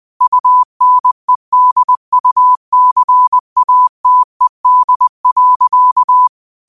Morse code, heard as a keyed tone, about 20 wpm, 1000 Hz. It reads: UNEDUCATED.
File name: qotdfiles / 20wpm